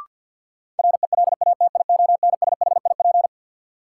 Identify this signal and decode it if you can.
Morse code, heard as a keyed tone, about 47 wpm, 700 Hz. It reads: RELATIONSHIP
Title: Sentences from Top 1100 Words - New Word Drill - 50wpm